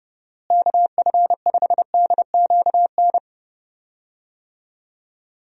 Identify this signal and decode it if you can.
Morse code, heard as a keyed tone, about 30 wpm, 700 Hz. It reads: KF5DQN
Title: USA Call Signs - 6 Characters - Repeated 2x - 30wpm